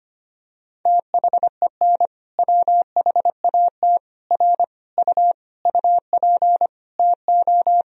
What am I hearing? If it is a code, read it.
Morse code, heard as a keyed tone, about 25 wpm, 700 Hz. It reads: THEN WHAT R U UP TO
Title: Sentences from Top 100 Words and Abbreviations 25wpm